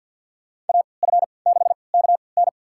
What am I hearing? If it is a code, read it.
Morse code, heard as a keyed tone, about 48 wpm, spaced as for 28 wpm, 700 Hz. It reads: AF6KN
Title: USA Call Signs - 5 Characters - 3x Character Spacing - 50wpm